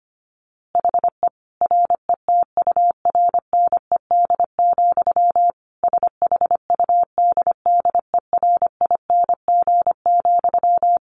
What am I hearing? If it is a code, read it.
Morse code, heard as a keyed tone, about 25 wpm, 700 Hz. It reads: HE RETURNED, SHUDDERING,
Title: A Sound Of Thunder 25wpm